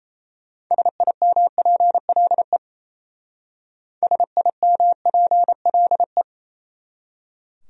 Morse code, heard as a keyed tone, about 35 wpm, 700 Hz. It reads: SIMPLE SIMPLE